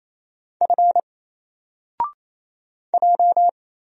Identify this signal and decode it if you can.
Morse code, heard as a keyed tone, about 27 wpm, 700 Hz. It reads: F J
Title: Single Character (Letter:Number:Punctuation:Prosign) 28wpm